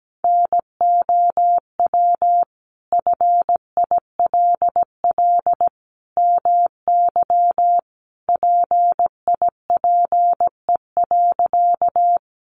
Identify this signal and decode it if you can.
Morse code, heard as a keyed tone, about 17 wpm, 700 Hz. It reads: NOW FILL MY PIPE.
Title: Sentences from Top 1500 Words 17wpm